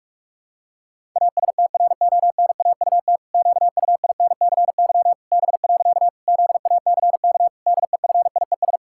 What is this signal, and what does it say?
Morse code, heard as a keyed tone, about 46 wpm, 700 Hz. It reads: ASTRONAUT QUINCY B. ZACK DEFIES
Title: Pangram Challenge v2 - 45wpm